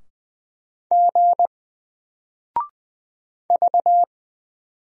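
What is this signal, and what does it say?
Morse code, heard as a keyed tone, about 20 wpm, 700 Hz. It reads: G V
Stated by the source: Single letter and number Morse drill at 20 wpm, from N4LCD.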